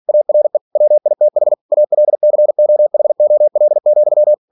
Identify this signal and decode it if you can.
Morse code, heard as a keyed tone, about 46 wpm, 600 Hz. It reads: ARE WITH ALCOHOL,
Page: Never 47wpm